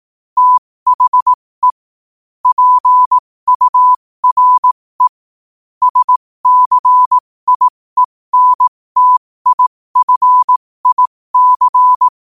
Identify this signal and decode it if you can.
Morse code, heard as a keyed tone, about 18 wpm, spaced as for 15 wpm, 1000 Hz. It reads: THE PURE SCIENTIFIC
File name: qotdfiles / 15wpm